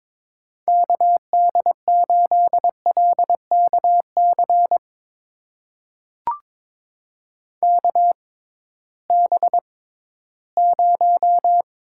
Morse code, heard as a keyed tone, about 22 wpm, spaced as for 14 wpm, 700 Hz. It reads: KD8LKC KB0
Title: USA Call Signs - 6 Characters - 6x Character Spacing - 22wpm